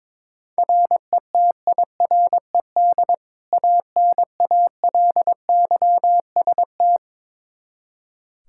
Morse code, heard as a keyed tone, about 22 wpm, 700 Hz. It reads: RETIRED ANALYST
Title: Practice QSO - Exchange 3 - More Info - 22wpm